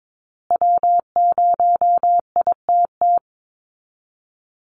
Morse code, heard as a keyed tone, about 22 wpm, 700 Hz. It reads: W0ITT
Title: Call Signs Repeated 2x - 22wpm